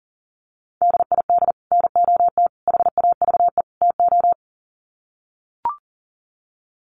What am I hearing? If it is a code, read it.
Morse code, heard as a keyed tone, about 37 wpm, 700 Hz. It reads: DID NOT HAVE TO
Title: Sets of 4 Words 40wpm